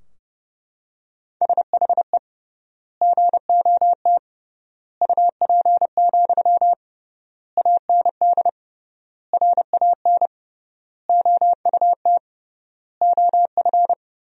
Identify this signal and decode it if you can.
Morse code, heard as a keyed tone, about 28 wpm, 700 Hz. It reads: SHE GOT UP, AND RAN OUT OF
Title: 3x Word-Spacing - Sentences from Top 500 Words - 30wpm